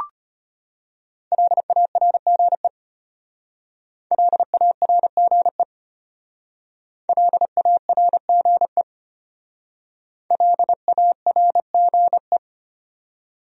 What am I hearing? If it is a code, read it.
Morse code, heard as a keyed tone, about 32 wpm, 700 Hz. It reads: LARGE LARGE LARGE LARGE